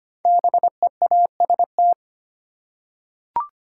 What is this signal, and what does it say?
Morse code, heard as a keyed tone, about 25 wpm, 700 Hz. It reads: BEAST
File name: 5 Letter Words - First 3 Letters Given - 25wpm